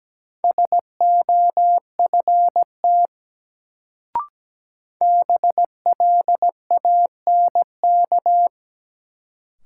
Morse code, heard as a keyed tone, about 17 wpm, 700 Hz. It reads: SOFT BLANK
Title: Sentences from Top 1500 Words - New Word Drill - 17wpm